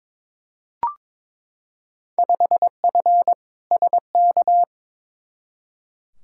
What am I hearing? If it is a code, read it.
Morse code, heard as a keyed tone, about 22 wpm, 700 Hz. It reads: 5F SK
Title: Field Day Exchanges Encore - 22wpm